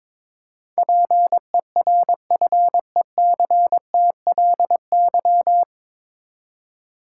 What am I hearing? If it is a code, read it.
Morse code, heard as a keyed tone, about 21 wpm, 700 Hz. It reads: PERFECTLY
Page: Morse Code Ninja practice 22wpm